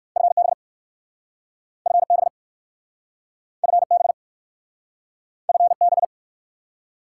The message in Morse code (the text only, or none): FB FB FB FB